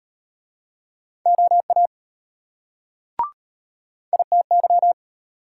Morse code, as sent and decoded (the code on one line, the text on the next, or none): --- .- / .. - -.--
OA ITY